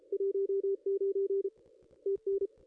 Morse code, heard as a keyed tone, about 31 wpm, 410 Hz, with radio noise. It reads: 19 TN
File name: contests / ss-cw-k5tr-op-n5rz -0500